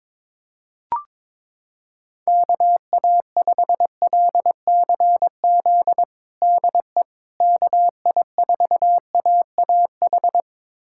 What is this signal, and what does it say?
Morse code, heard as a keyed tone, about 22 wpm, 700 Hz. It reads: KA5LCZ DE KI4AAH
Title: USA Callsign DE Callsign - 6 Characters - 22wpm